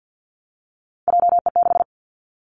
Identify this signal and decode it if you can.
Morse code, heard as a keyed tone, about 50 wpm, 700 Hz. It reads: JE6
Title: Three Challenging Letter-Number Combinations 50wpm